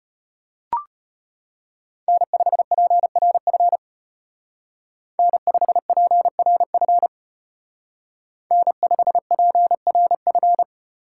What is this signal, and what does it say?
Morse code, heard as a keyed tone, about 30 wpm, 700 Hz. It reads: N5PRF N5PRF N5PRF
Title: Speed Racing - USA 1x3 Call Signs 25wpm